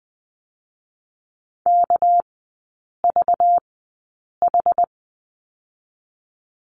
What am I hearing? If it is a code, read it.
Morse code, heard as a keyed tone, about 20 wpm, 700 Hz. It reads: K V H
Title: 198 Lesson 32e - Letter k - Three Characters 20wpm